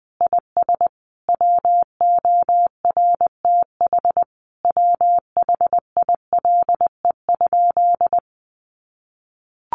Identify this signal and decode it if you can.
Morse code, heard as a keyed tone, about 20 wpm, 700 Hz. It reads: IS WORTH WHILE?
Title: Sentences from Top 800 Words 20wpm